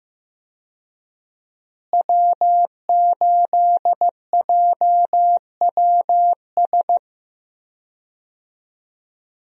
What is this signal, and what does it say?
Morse code, heard as a keyed tone, about 15 wpm, 700 Hz. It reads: W8JWS
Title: USA Call Signs - 5 Characters - Repeated 2x - 15wpm